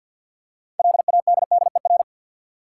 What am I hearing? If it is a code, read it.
Morse code, heard as a keyed tone, about 51 wpm, 700 Hz. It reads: LADDER
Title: Six Letter Words - 50wpm